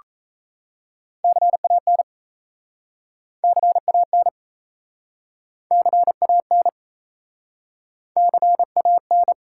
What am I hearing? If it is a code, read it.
Morse code, heard as a keyed tone, about 37 wpm, 700 Hz. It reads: CAN CAN CAN CAN